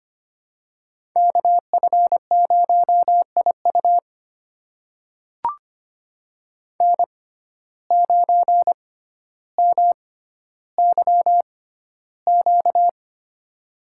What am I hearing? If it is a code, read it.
Morse code, heard as a keyed tone, about 25 wpm, spaced as for 13 wpm, 700 Hz. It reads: KF0IU N9MYQ
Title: USA Call Signs - 5 Characters - 6x Character Spacing - 25wpm